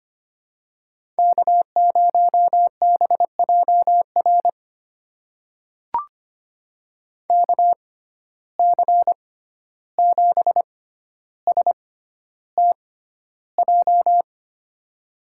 Morse code, heard as a keyed tone, about 25 wpm, spaced as for 12 wpm, 700 Hz. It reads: K0BJR KC7STJ
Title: USA Call Signs - 6x Character Spacing - 25wpm